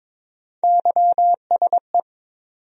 Morse code, heard as a keyed tone, about 22 wpm, 700 Hz. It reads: YSE
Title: USA Call Sign Postfixes - 3 Letters 22wpm